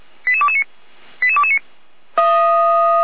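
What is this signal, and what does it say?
Five-tone selective call (ZVEI), heard as quick runs of five tones, tones of about 70 ms. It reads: 88229 88229
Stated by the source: ZVEI five-tone selective call 88102, received via SDR, from the Feuerwehr Orlamünde fire brigade audio server.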